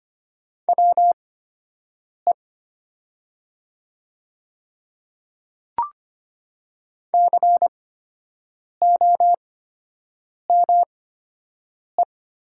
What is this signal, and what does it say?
Morse code, heard as a keyed tone, about 25 wpm, spaced as for 7 wpm, 700 Hz. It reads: WE COME
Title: 8x Character Spacing - Top 100 Words - 25wpm